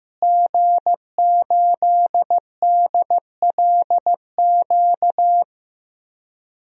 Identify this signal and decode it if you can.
Morse code, heard as a keyed tone, about 15 wpm, 700 Hz. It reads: G8DLQ
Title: DXCC-100 Call Signs - 5 Characters - Repeated 2x - 15wpm